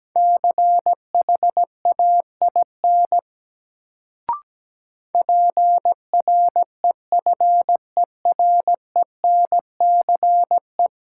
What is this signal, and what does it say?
Morse code, heard as a keyed tone, about 17 wpm, 700 Hz. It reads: CHAIN PREFERENCE
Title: Sentences from Top 1500 Words - New Word Drill - 17wpm